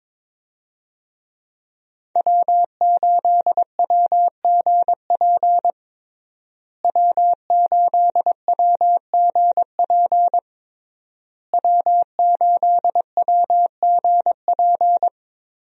Morse code, heard as a keyed tone, about 22 wpm, 700 Hz. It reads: W8WGP W8WGP W8WGP
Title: USA Call Signs - 5 Characters - Repeated 3x - 22wpm